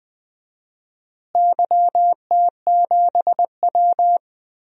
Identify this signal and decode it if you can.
Morse code, heard as a keyed tone, about 21 wpm, 700 Hz. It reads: YT7W
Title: DXCC-100 Call Signs - 4 Characters - Repeated 3x - 20wpm